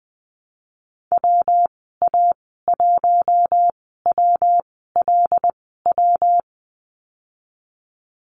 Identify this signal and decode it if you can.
Morse code, heard as a keyed tone, about 20 wpm, spaced as for 15 wpm, 700 Hz. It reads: WA1WLW
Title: 179 Lesson 29g - Slash - Call Signs 20wpm